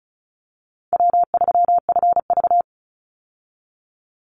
Morse code, heard as a keyed tone, about 34 wpm, 700 Hz. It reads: W3FV
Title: Easy Call Signs Repeated 3x - 35wpm